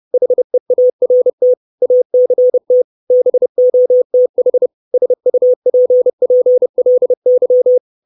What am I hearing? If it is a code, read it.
Morse code, heard as a keyed tone, about 30 wpm, 500 Hz. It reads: HEART ACT BOTH SUPPLY
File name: 632 Words 30WPM